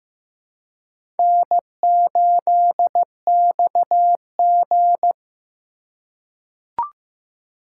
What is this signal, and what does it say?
Morse code, heard as a keyed tone, about 15 wpm, 700 Hz. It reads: N8XG
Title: USA Call Signs - 4 Characters - 4x Character Spacing - 15wpm